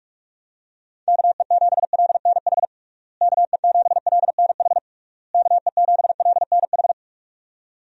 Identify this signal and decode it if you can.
Morse code, heard as a keyed tone, about 44 wpm, 700 Hz. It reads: KE7LNH KE7LNH KE7LNH
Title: USA Call Signs - 6 Characters - Repeated 3x - 45wpm